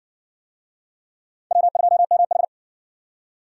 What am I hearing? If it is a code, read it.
Morse code, heard as a keyed tone, about 60 wpm, 700 Hz. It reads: W2MH